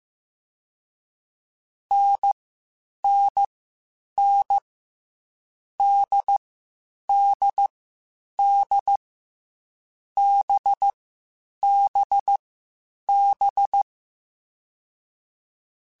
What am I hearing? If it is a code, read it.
Morse code, heard as a keyed tone, about 15 wpm, 800 Hz. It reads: N N N D D D B B B
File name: Short Runs / Short Runs-mp3s / NDB NDB6 @13